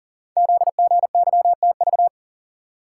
Morse code, heard as a keyed tone, about 40 wpm, 700 Hz. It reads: ZGYTV
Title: Code Groups - 5 Letters - 40wpm